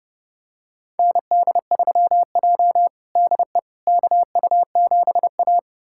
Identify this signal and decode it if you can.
Morse code, heard as a keyed tone, about 32 wpm, 700 Hz. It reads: ND3J DE KU7A
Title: USA Callsign DE Callsign - 4 Characters - 30wpm